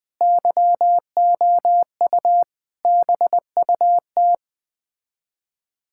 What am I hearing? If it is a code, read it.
Morse code, heard as a keyed tone, about 20 wpm, 700 Hz. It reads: YOU BUT
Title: Sets of 8 Words - 20wpm